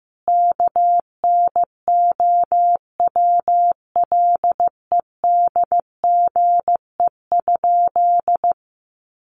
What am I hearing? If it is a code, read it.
Morse code, heard as a keyed tone, about 14 wpm, 700 Hz. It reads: KNOWLEDGE?
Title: Sentences from Top 600 Words 15wpm